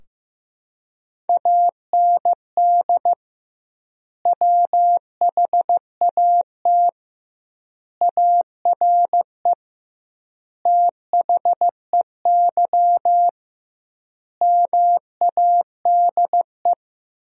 Morse code, heard as a keyed tone, about 16 wpm, 700 Hz. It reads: AND WHAT ARE THEY MADE
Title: 2x Word-Spacing - Sentences from Top 200 Words - 15wpm